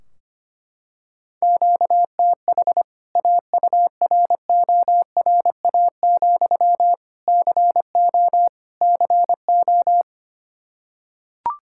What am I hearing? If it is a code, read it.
Morse code, heard as a keyed tone, about 25 wpm, 700 Hz. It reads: QTH AURORA, CO CO
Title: K1USN SST QSO Practice 25wpm